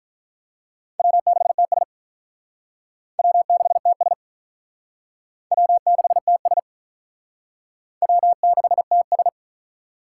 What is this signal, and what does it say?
Morse code, heard as a keyed tone, about 46 wpm, 700 Hz. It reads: W6TS W6TS W6TS W6TS